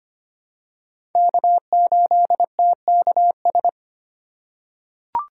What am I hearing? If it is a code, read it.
Morse code, heard as a keyed tone, about 25 wpm, 700 Hz. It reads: K8TKS
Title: USA Call Signs - 5 Characters - 6x Character Spacing - 25wpm